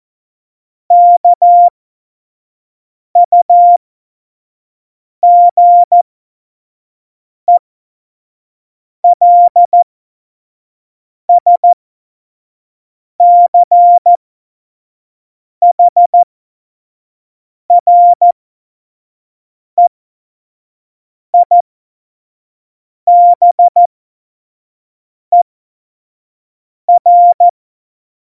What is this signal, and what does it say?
Morse code, heard as a keyed tone, about 14 wpm, spaced as for 5 wpm, 700 Hz. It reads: KUGELSCHREIBER
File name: Morsekurs / Witze / Witz 30. April 2019, 25 F70